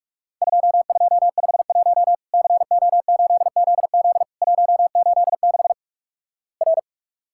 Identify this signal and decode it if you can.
Morse code, heard as a keyed tone, about 46 wpm, 700 Hz. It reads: 2251 CO8ZZ 186 R
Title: Worked All Europe DX Contest - QTC Exchange - 45wpm